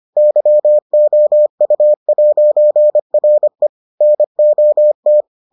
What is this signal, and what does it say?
Morse code, heard as a keyed tone, about 25 wpm, 600 Hz. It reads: YOU'RE NOT